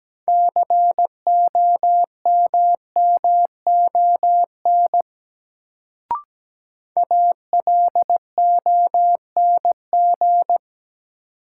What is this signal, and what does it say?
Morse code, heard as a keyed tone, about 17 wpm, 700 Hz. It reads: COMMON ALONG
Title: Sentences from Top 700 Words - New Word Drill - 17wpm